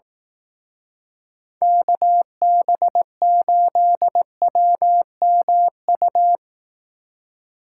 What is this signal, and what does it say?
Morse code, heard as a keyed tone, about 18 wpm, 700 Hz. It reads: KB8WMU
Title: Speed Racing - USA 2x3 Call Signs 15wpm